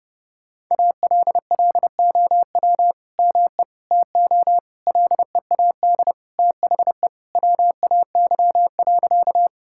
Morse code, heard as a keyed tone, about 29 wpm, 700 Hz. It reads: ALLOW ME TO LEAD THE WAY.
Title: Sentences from Top 1000 Words - Review Part 5 - 30wpm